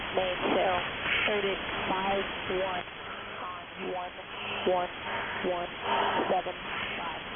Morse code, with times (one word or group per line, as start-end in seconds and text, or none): none